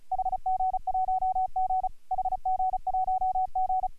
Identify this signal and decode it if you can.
Morse code, heard as a keyed tone, about 33 wpm, 750 Hz. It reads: HG1G HG1G